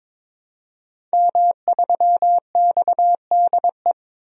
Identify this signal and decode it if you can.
Morse code, heard as a keyed tone, about 22 wpm, 700 Hz. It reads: M3XDE
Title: DXCC-100 Call Signs - 5 Characters - Repeated 2x - 22wpm